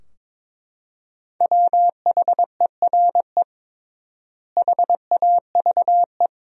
Morse code, heard as a keyed tone, about 22 wpm, 700 Hz. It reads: WHERE HAVE
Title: 3x Word-Spacing - Sentences from Top 200 Words - 22wpm